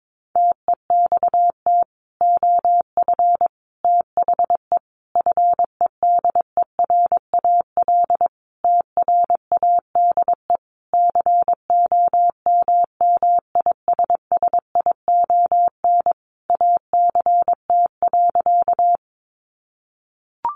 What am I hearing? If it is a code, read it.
Morse code, heard as a keyed tone, about 22 wpm, 700 Hz. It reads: TEXT OF THE FEDERAL TRADE COMMISSION ACT.
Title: Sentences from Top 1400 Words 22wpm